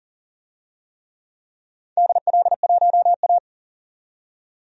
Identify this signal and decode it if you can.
Morse code, heard as a keyed tone, about 40 wpm, 690 Hz. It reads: DL1A
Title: DXCC-100 Call Signs - 4 Characters - 40wpm